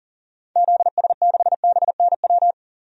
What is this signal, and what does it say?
Morse code, heard as a keyed tone, about 40 wpm, 700 Hz. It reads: ZS6BNW
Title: DXCC-100 Call Signs - 6 Characters - 40wpm